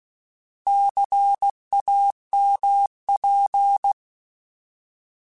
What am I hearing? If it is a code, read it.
Morse code, heard as a keyed tone, about 16 wpm, 800 Hz. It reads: CAMP
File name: Alphabetical / Alphabetical mp3s / Cs @15